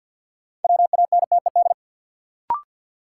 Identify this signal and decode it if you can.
Morse code, heard as a keyed tone, about 54 wpm, 700 Hz. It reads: WANTED
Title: Sets of 2 Words - New Word Drill - 50wpm